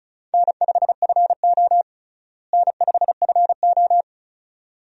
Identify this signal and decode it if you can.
Morse code, heard as a keyed tone, about 36 wpm, 700 Hz. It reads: N5FO N5FO
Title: USA Call Signs - 4 Characters - Repeated 2x - 35wpm